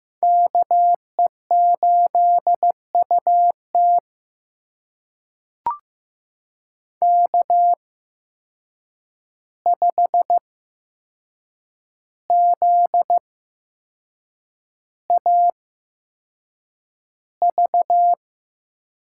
Morse code, heard as a keyed tone, about 15 wpm, spaced as for 6 wpm, 700 Hz. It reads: KE8UT K5ZAV
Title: USA Call Signs - 5 Characters - 8x Character Spacing - 15wpm